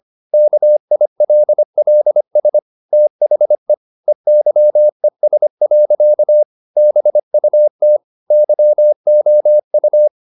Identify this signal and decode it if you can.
Morse code, heard as a keyed tone, about 25 wpm, 600 Hz. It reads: KILLS THE EYES. BUT YOU
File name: the old man and the sea part 1 25wpm